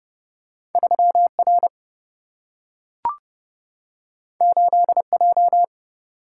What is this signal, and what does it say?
Morse code, heard as a keyed tone, about 30 wpm, 700 Hz. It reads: 3 R 8 J
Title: Single-Letter Single-Number 30wpm